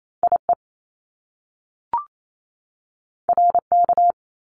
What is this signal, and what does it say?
Morse code, heard as a keyed tone, about 28 wpm, 700 Hz. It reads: IE RK
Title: Top Two Letter Combinations in Words 28wpm